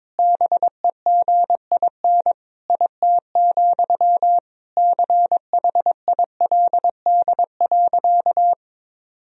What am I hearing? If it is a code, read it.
Morse code, heard as a keyed tone, about 21 wpm, 700 Hz. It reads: BEGIN IT, CHILD.
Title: Sentences from Top 1000 Words - Review Part 1 - 22wpm